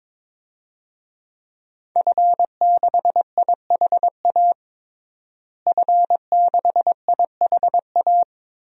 Morse code, heard as a keyed tone, about 22 wpm, 700 Hz. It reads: F6IHA F6IHA
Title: DXCC-100 Call Signs - 5 Characters - Repeated 3x - 22wpm